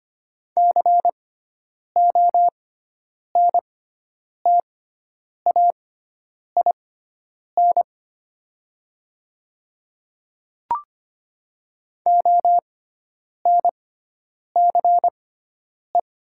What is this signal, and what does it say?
Morse code, heard as a keyed tone, about 25 wpm, spaced as for 9 wpm, 700 Hz. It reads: CONTAIN ONCE
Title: 6x Character Spacing - Top 500 Words - 25wpm